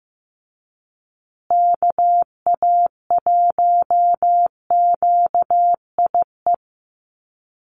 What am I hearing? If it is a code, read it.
Morse code, heard as a keyed tone, about 15 wpm, 700 Hz. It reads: KA1QIE
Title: USA Call Signs Repeated 3x 15wpm